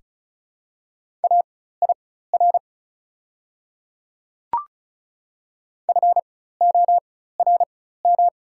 Morse code, heard as a keyed tone, about 36 wpm, spaced as for 16 wpm, 690 Hz. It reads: AIR FORM
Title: 4x Character Spacing - Top 500 Words - 35wpm